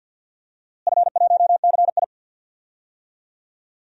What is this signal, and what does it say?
Morse code, heard as a keyed tone, about 50 wpm, 700 Hz. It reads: F1CI
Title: DXCC-100 Call Signs - 4 Characters - Repeated 3x - 50wpm